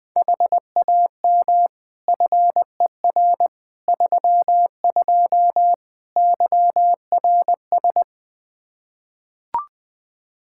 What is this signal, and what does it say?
Morse code, heard as a keyed tone, about 20 wpm, 700 Hz. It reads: HAM FER 32 YRS
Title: Practice QSO - Exchange 3 - More Info - 20wpm